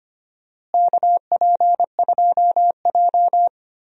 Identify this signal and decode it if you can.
Morse code, heard as a keyed tone, about 25 wpm, 700 Hz. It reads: KP2J
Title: DXCC-100 Call Signs - 4 Characters - Repeated 2x - 25wpm